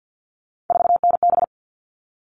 Morse code, heard as a keyed tone, about 50 wpm, 700 Hz. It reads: 4NB
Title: All Cut Numbers - 3-Digit 50wpm